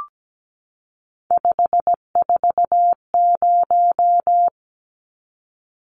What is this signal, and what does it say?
Morse code, heard as a keyed tone, about 17 wpm, 700 Hz. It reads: 540
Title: Numbers - 3-Digit 17wpm